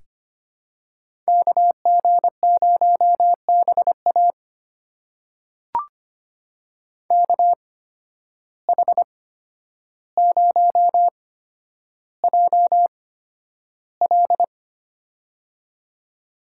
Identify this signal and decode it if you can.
Morse code, heard as a keyed tone, about 25 wpm, spaced as for 11 wpm, 700 Hz. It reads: KG0BA KH0JL
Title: USA Call Signs - 5 Characters - 8x Character Spacing - 25wpm